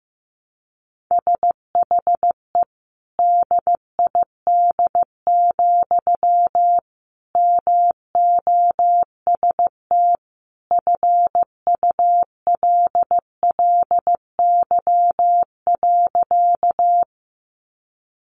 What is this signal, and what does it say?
Morse code, heard as a keyed tone, about 14 wpm, 700 Hz. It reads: SHE DID, MOST FULLY.
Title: Sentences from Top 1300 Words 15wpm